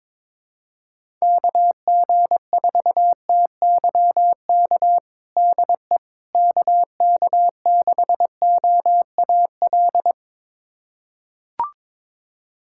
Morse code, heard as a keyed tone, about 22 wpm, 700 Hz. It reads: KG4TYK DE KK6OAL
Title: USA Callsign DE Callsign - 6 Characters - 22wpm